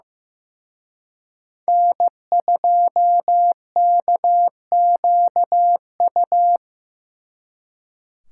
Morse code, heard as a keyed tone, about 18 wpm, 700 Hz. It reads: N2KQU
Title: Speed Racing - USA 1x3 Call Signs 15wpm